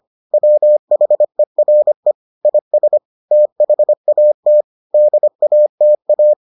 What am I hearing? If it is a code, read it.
Morse code, heard as a keyed tone, about 25 wpm, 600 Hz. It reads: WHERE IS THAT DATA